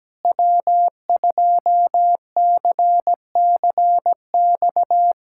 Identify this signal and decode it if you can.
Morse code, heard as a keyed tone, about 17 wpm, 700 Hz. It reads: W2CCX
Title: USA Callsign DE Callsign - 5 Characters - 17wpm